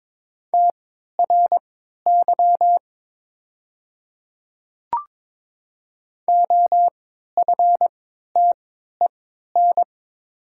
Morse code, heard as a keyed tone, about 22 wpm, spaced as for 13 wpm, 700 Hz. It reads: TRY OFTEN
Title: 3x Character Spacing - Top 300 Words - 22wpm